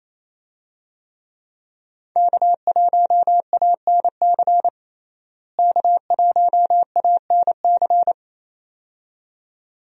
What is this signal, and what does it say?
Morse code, heard as a keyed tone, about 28 wpm, 700 Hz. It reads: K1ANC K1ANC
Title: USA Call Signs - 5 Characters - Repeated 2x - 28wpm